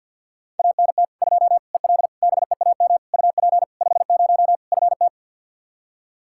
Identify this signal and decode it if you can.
Morse code, heard as a keyed tone, about 50 wpm, 700 Hz. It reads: ANT 2 EL BEAM UP 50 FT
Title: K1USN SST QSO Practice 50wpm